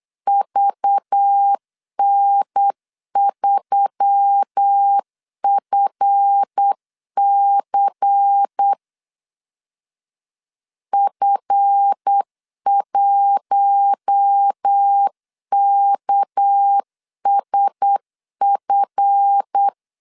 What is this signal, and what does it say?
Morse code, heard as a keyed tone, about 8 wpm, 800 Hz. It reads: VN3FC F1KSF
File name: LEZIONI / 7 GF9Z / 7 GF 8wpm 3108